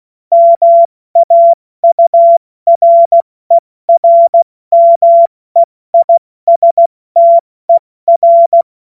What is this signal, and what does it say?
Morse code, heard as a keyed tone, about 16 wpm, 680 Hz. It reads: MAURERMEISTER
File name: Morsekurs / Witze / Witz 17.Oktober 2023 70 F80